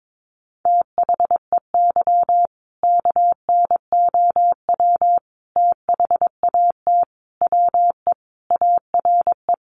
Morse code, heard as a keyed tone, about 22 wpm, 700 Hz. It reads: THEY KNOW THAT WE ARE
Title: Sentences from Top 1400 Words 22wpm